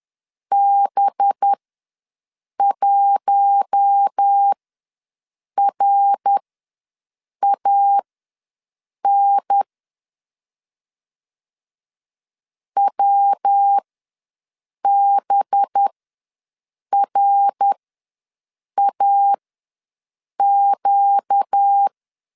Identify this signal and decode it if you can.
Morse code, heard as a keyed tone, about 10 wpm, spaced as for 6 wpm, 800 Hz. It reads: B1RAN WBRAQ
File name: LEZIONI / 5 LBVC / 5 LB ws 10wpm 3616